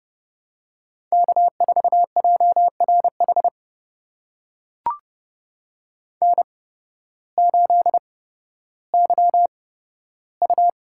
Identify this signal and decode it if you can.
Morse code, heard as a keyed tone, about 30 wpm, spaced as for 14 wpm, 700 Hz. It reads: K4JRH N8YU
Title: USA Call Signs - 5 Characters - 8x Character Spacing - 30wpm